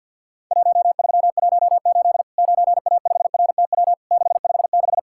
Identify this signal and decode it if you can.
Morse code, heard as a keyed tone, about 50 wpm, 700 Hz. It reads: 1318 9A5RTW 656
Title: Worked All Europe DX Contest - QTC Exchange - 50wpm